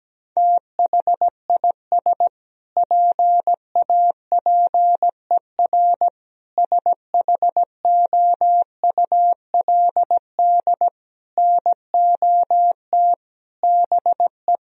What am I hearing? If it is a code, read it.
Morse code, heard as a keyed tone, about 17 wpm, 700 Hz. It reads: THIS PAPER SHOULD NOT BE
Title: Sentences from Top 1000 Words - Review Part 3 - 17wpm